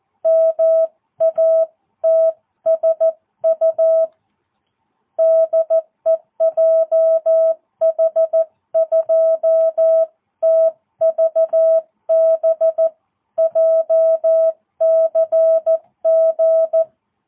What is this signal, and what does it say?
Morse code, heard as a keyed tone, about 14 wpm, 650 Hz. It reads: MATSU DE JH2TVB JCG
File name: jh8cbh / 2022cw